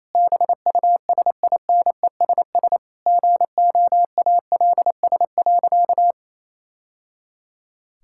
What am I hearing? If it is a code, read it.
Morse code, heard as a keyed tone, about 28 wpm, 700 Hz. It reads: BUSINESS GOALS.
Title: Sentences from Top 1000 Words - Review Part 6 - 28wpm